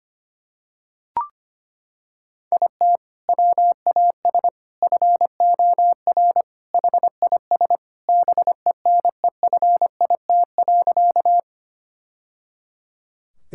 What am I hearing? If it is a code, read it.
Morse code, heard as a keyed tone, about 25 wpm, 700 Hz. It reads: IT WAS FOR HIS BENEFIT.
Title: Sentences from Top 1000 Words - Review Part 6 - 25wpm